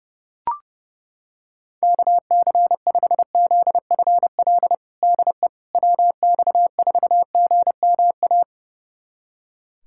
Morse code, heard as a keyed tone, about 30 wpm, 700 Hz. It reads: KC5ZFL DE WX4GMA
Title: USA Callsign DE Callsign - 6 Characters - 30wpm